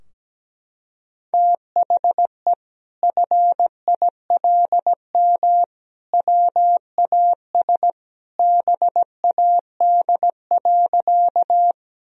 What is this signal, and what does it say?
Morse code, heard as a keyed tone, about 17 wpm, 700 Hz. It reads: THE FILM WAS BAD.
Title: Sentences from Top 1000 Words - Review Part 3 - 17wpm